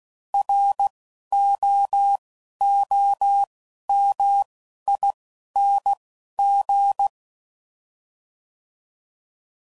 Morse code, heard as a keyed tone, about 16 wpm, 800 Hz. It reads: R O O M I N G
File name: Prefixes / By First Letter / Rs